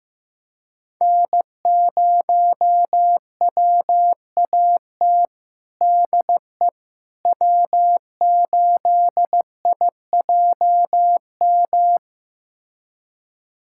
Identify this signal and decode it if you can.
Morse code, heard as a keyed tone, about 16 wpm, 700 Hz. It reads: N0WAT DE W8IJM
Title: USA Callsign DE Callsign - 5 Characters - 15wpm